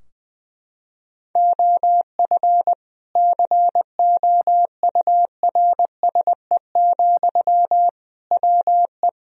Morse code, heard as a keyed tone, about 21 wpm, 700 Hz. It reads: OF COURSE, WE
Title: Sentences from Top 1000 Words - Review Part 2 - 20wpm